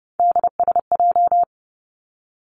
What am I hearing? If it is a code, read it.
Morse code, heard as a keyed tone, about 31 wpm, 700 Hz. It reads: DSJ
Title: Three Challenging Letter-Number Combinations 30wpm